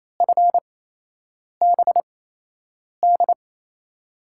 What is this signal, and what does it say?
Morse code, heard as a keyed tone, about 28 wpm, spaced as for 8 wpm, 700 Hz. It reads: FBD
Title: USA Call Signs - 8x Character Spacing - 28wpm